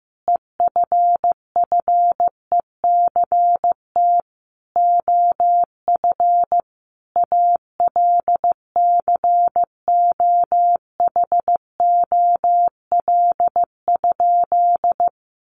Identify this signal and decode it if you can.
Morse code, heard as a keyed tone, about 15 wpm, 700 Hz. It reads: EFFECT OF ALCOHOL?
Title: Sentences from Top 1400 Words 15wpm